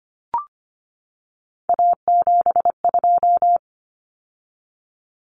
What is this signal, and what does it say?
Morse code, heard as a keyed tone, about 25 wpm, 700 Hz. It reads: A72
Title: Common Cut Numbers - 3-Digit 25wpm